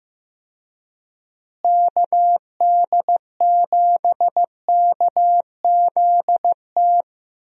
Morse code, heard as a keyed tone, about 16 wpm, 700 Hz. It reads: KD7KZT
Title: USA Call Signs - 6 Characters - Repeated 2x - 15wpm